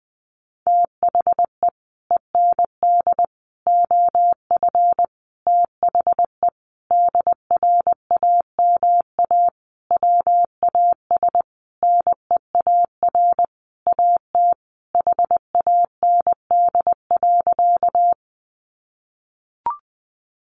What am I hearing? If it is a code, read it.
Morse code, heard as a keyed tone, about 20 wpm, 700 Hz. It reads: THE END OF THE DRAMA WAS NEAR AT HAND.
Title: Sentences from Top 1500 Words 20wpm